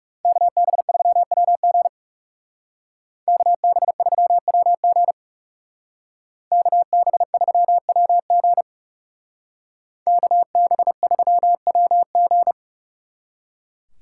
Morse code, heard as a keyed tone, about 37 wpm, 700 Hz. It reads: KB3WG KB3WG KB3WG KB3WG
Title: Speed Racing - USA 2x2 Call Signs 30wpm